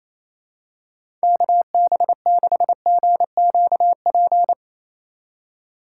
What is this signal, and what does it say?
Morse code, heard as a keyed tone, about 28 wpm, 700 Hz. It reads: KB6GQP
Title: USA Call Signs - 6 Characters - 6x Character Spacing - 28wpm